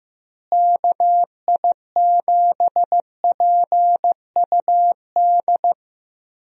Morse code, heard as a keyed tone, about 15 wpm, 700 Hz. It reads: KI7PUD
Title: USA Call Signs - 6 Characters - Repeated 2x - 15wpm